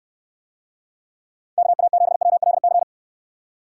Morse code, heard as a keyed tone, about 69 wpm, 700 Hz. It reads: KA7WLC